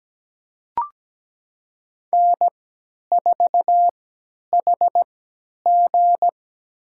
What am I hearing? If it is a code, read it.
Morse code, heard as a keyed tone, about 17 wpm, spaced as for 10 wpm, 700 Hz. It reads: N4HG
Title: USA Call Signs - 3x Character Spacing - 17wpm